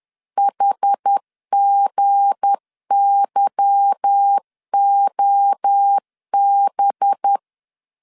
Morse code, heard as a keyed tone, about 10 wpm, 800 Hz. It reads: HGYOB
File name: LEZIONI / 8 YJ45 / 8 Y 10wpm 2615